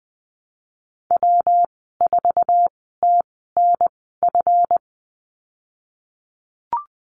Morse code, heard as a keyed tone, about 20 wpm, spaced as for 15 wpm, 700 Hz. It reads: W4TNF
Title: 161 Lesson 26g - Letter g - Call Signs 20wpm